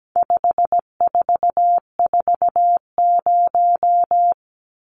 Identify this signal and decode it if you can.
Morse code, heard as a keyed tone, about 17 wpm, 700 Hz. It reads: 5440
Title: Numbers - 4-Digit 17wpm